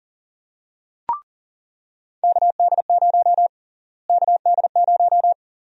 Morse code, heard as a keyed tone, about 38 wpm, 700 Hz. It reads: KD0 KD0
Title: USA Call Sign Prefixes - Repeated 2x - 40wpm